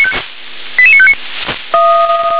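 Five-tone selective call (ZVEI), heard as quick runs of five tones, tones of about 70 ms. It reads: 88959
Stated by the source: ZVEI five-tone call received by SDR.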